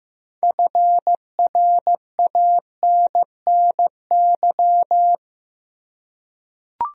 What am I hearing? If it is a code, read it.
Morse code, heard as a keyed tone, about 16 wpm, 700 Hz. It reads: FRANNY
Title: USA Common Names - 15wpm